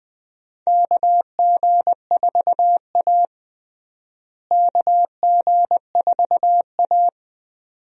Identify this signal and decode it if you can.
Morse code, heard as a keyed tone, about 21 wpm, 700 Hz. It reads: KG4A KG4A
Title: USA Call Signs - 4 Characters - Repeated 2x - 20wpm